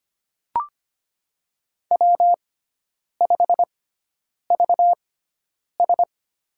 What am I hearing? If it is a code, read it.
Morse code, heard as a keyed tone, about 25 wpm, spaced as for 9 wpm, 700 Hz. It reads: W5VS
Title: USA Call Signs - 6x Character Spacing - 25wpm